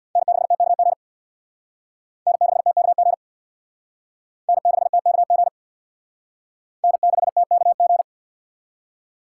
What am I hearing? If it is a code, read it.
Morse code, heard as a keyed tone, about 62 wpm, 700 Hz. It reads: N6TKG N6TKG N6TKG N6TKG
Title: Speed Racing - USA 1x3 Call Signs 50wpm